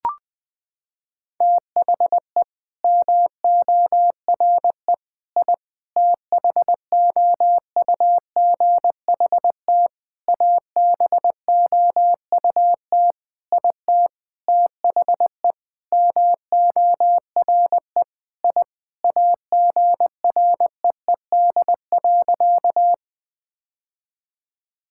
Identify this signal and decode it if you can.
Morse code, heard as a keyed tone, about 20 wpm, 700 Hz. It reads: THE MORE I THOUGHT ABOUT IT THE MORE I AGREED.
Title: Sentences with 9-Word Phrases - 20wpm